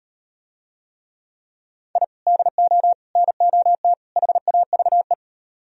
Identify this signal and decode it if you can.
Morse code, heard as a keyed tone, about 38 wpm, 700 Hz. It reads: I DO NOT HAVE